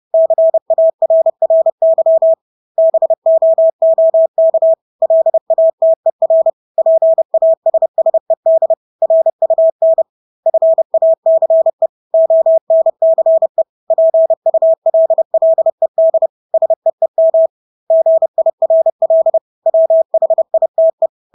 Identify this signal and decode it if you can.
Morse code, heard as a keyed tone, about 30 wpm, 650 Hz. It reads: CARRY BOOK LATER PASSED RUN FACE ONCE PULLED SEEM GIRL WHITE
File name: Engelske ord 30-30 wpm